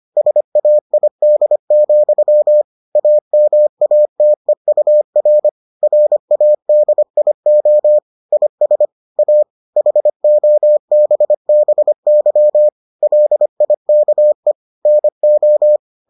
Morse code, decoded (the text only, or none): SAID, AMATEUR RADIO IS A HOBBY LIKE NO